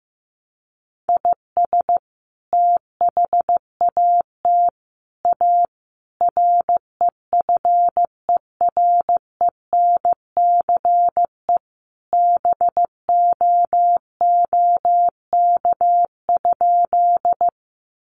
Morse code, heard as a keyed tone, about 15 wpm, 700 Hz. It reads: IS THAT A REFERENCE BOOK?
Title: Sentences from Top 1200 Words 15wpm